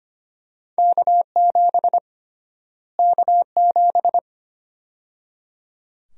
Morse code, heard as a keyed tone, about 25 wpm, 700 Hz. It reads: K7 K7
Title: USA Call Sign Prefixes - Repeated 2x - 25wpm